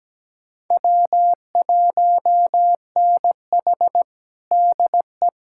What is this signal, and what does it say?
Morse code, heard as a keyed tone, about 17 wpm, 700 Hz. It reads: W1NH DE
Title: USA Callsign DE Callsign - 4 Characters - 17wpm